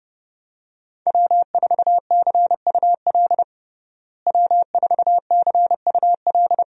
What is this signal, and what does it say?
Morse code, heard as a keyed tone, about 30 wpm, 700 Hz. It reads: W4CUL W4CUL
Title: USA Call Signs - 5 Characters - Repeated 2x - 30wpm